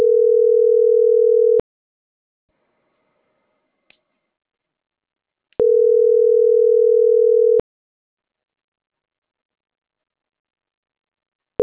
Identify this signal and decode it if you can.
Ringback tone (North America), 440 + 480 Hz.